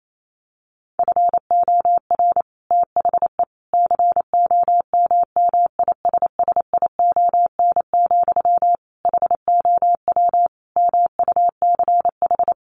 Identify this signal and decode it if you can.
Morse code, heard as a keyed tone, about 28 wpm, 700 Hz. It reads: FOR THE COMMISSION, HOW MUCH
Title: Sentences from Top 1400 Words 28wpm